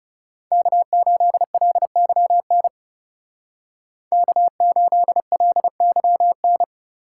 Morse code, heard as a keyed tone, about 38 wpm, 700 Hz. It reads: K8LYN K8LYN
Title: Speed Racing - USA 1x3 Call Signs 30wpm